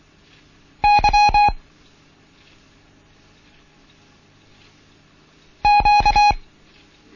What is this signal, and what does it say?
Morse code, heard as a keyed tone, about 24 wpm, 830 Hz. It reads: Y Q
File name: Morsen / Kurs A 16